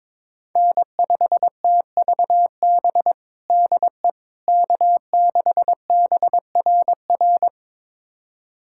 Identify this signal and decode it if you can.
Morse code, heard as a keyed tone, about 21 wpm, 700 Hz. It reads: N5TVB DE K6BRR
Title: USA Callsign DE Callsign - 5 Characters - 22wpm